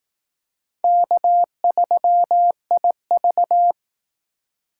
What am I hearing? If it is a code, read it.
Morse code, heard as a keyed tone, about 18 wpm, 700 Hz. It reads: K3IV
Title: Speed Racing - USA 1x2 Call Signs 15wpm